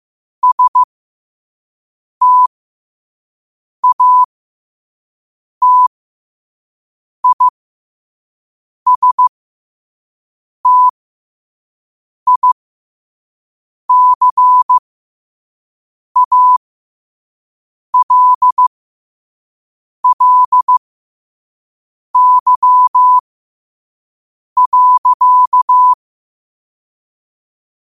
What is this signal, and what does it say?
Morse code, heard as a keyed tone, about 15 wpm, spaced as for 5 wpm, 1000 Hz. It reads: STATISTICALLY.
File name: qotdfiles / 5wpm